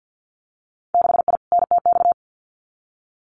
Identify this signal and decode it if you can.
Morse code, heard as a keyed tone, about 50 wpm, 700 Hz. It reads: 6I NTX
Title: Winter Field Day Exchanges 50wpm